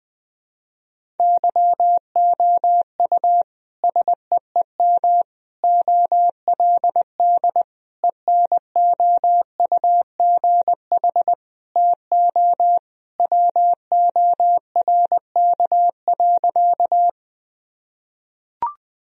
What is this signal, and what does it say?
Morse code, heard as a keyed tone, about 20 wpm, 700 Hz. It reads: YOU SEEM OLD ENOUGH TO WORK.
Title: Sentences from Top 1000 Words - Review Part 1 - 20wpm